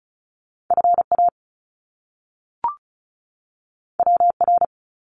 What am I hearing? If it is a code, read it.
Morse code, heard as a keyed tone, about 36 wpm, 700 Hz. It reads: FA WR
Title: Top Two Letter Combinations in Words 35wpm